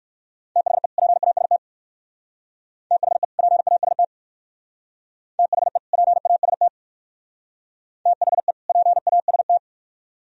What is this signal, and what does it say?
Morse code, heard as a keyed tone, about 58 wpm, 700 Hz. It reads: THE PAST THE PAST THE PAST THE PAST